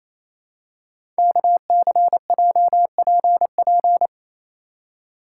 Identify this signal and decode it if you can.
Morse code, heard as a keyed tone, about 29 wpm, 700 Hz. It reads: KCJPP
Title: Code Groups - 5 Letters - 28wpm